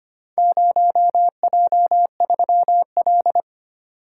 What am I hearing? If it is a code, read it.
Morse code, heard as a keyed tone, about 25 wpm, 700 Hz. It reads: 0J3L
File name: practice / 25wpm / Four Letter-Number Combinations - 25wpm